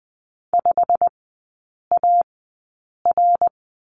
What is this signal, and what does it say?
Morse code, heard as a keyed tone, about 20 wpm, spaced as for 8 wpm, 690 Hz. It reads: 5AR
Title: 087 Lesson 15e - Number 5 - Three Characters 20wpm